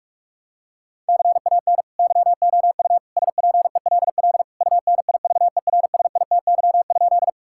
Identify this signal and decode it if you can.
Morse code, heard as a keyed tone, about 46 wpm, 690 Hz. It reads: CAN YOU SPELL UNIVERSITY?
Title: Sentences from Top 1000 Words - Review Part 5 - 45wpm